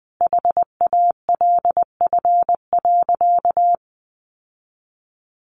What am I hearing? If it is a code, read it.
Morse code, heard as a keyed tone, about 20 wpm, 700 Hz. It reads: HALF.
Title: Sentences from Top 800 Words 20wpm